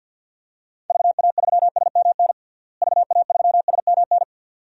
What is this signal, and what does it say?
Morse code, heard as a keyed tone, about 52 wpm, 700 Hz. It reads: VA3SMN VA3SMN
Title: DXCC-100 Call Signs - 6 Characters - Repeated 2x - 50wpm